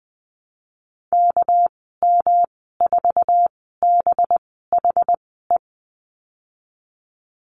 Morse code, heard as a keyed tone, about 19 wpm, spaced as for 14 wpm, 700 Hz. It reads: KM4BHE
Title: 200 Lesson 32g - Letter k - Call Signs 20wpm